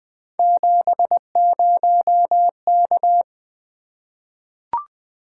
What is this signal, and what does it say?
Morse code, heard as a keyed tone, about 20 wpm, 700 Hz. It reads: 70K
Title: Three Letter-Number Combinations - 20wpm